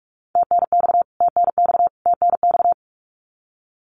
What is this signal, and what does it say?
Morse code, heard as a keyed tone, about 46 wpm, 700 Hz. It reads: TNX TNX TNX
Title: Abbreviations Repeated 3x - 45wpm